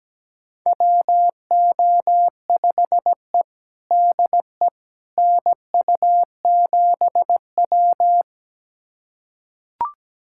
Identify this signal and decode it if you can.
Morse code, heard as a keyed tone, about 17 wpm, 700 Hz. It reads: WO5E DE NU7W
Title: USA Callsign DE Callsign - 4 Characters - 17wpm